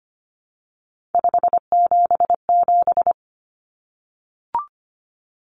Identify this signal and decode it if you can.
Morse code, heard as a keyed tone, about 25 wpm, 700 Hz. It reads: 577
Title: Common Cut Numbers - 3-Digit 25wpm